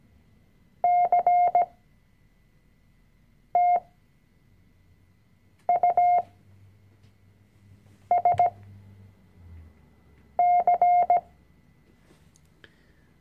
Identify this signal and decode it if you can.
Morse code, heard as a keyed tone, about 17 wpm, spaced as for 4 wpm, 710 Hz. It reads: CTUSC
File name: CW-Online-Kurs / Anfänger / Lektion 005